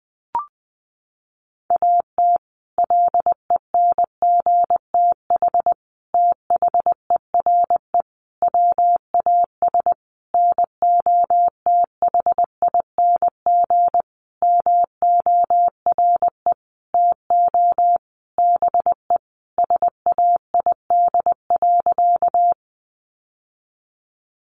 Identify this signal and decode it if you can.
Morse code, heard as a keyed tone, about 20 wpm, 700 Hz. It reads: AT LENGTH THERE WAS NOTHING MORE TO BE SAID.
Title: Sentences from Top 1000 Words 20wpm